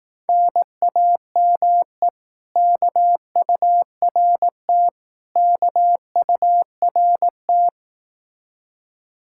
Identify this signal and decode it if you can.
Morse code, heard as a keyed tone, about 18 wpm, 700 Hz. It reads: NAME KURT KURT
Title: Speed-Racing Common QSO Phrases Encore - 15wpm